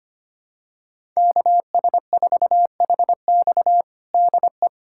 Morse code, heard as a keyed tone, about 25 wpm, 700 Hz. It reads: KS4HX DE
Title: USA Callsign DE Callsign - 5 Characters - 25wpm